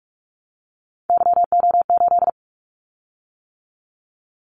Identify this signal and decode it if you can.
Morse code, heard as a keyed tone, about 46 wpm, 700 Hz. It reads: YO8
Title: Callsign Prefixes for Top 25 DXCC - 45wpm